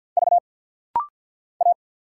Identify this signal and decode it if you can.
Morse code, heard as a keyed tone, about 50 wpm, 700 Hz. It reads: V A
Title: Single Letter-Number Alt - Rapid-Fire 50wpm